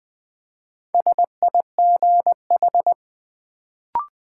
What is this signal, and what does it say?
Morse code, heard as a keyed tone, about 20 wpm, 700 Hz. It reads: SIGH
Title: Build A Word - 2 Letters At A Time - 4-Letter Words - 20wpm